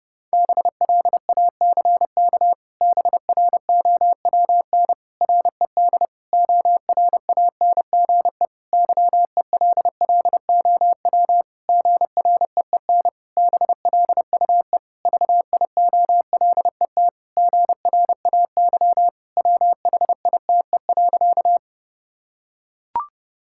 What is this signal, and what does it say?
Morse code, heard as a keyed tone, about 30 wpm, 700 Hz. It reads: BLACK BROWN RED ORANGE YELLOW GREEN BLUE VIOLET GRAY WHITE.
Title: Pangram Challenge v2 - 30wpm